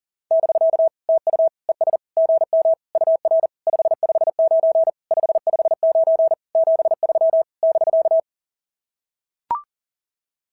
Morse code, heard as a keyed tone, about 37 wpm, 660 Hz. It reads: <BK> TU ES GM UR 559 559 73 <BK>
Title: SOTA QSO Practice 40wpm